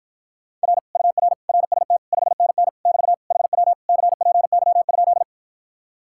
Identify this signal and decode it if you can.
Morse code, heard as a keyed tone, about 54 wpm, 700 Hz. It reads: R UR RST 5NN <BT> HW CPY?